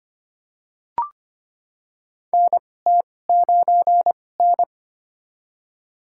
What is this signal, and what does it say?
Morse code, heard as a keyed tone, about 25 wpm, spaced as for 18 wpm, 700 Hz. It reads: NT9N
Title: USA Call Signs - 4 Characters - 2x Character Spacing - 25wpm